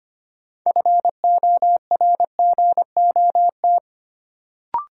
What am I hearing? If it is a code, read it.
Morse code, heard as a keyed tone, about 25 wpm, 700 Hz. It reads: FORGOT
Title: Sentences from Top 1400 Words - New Word Drill - 25wpm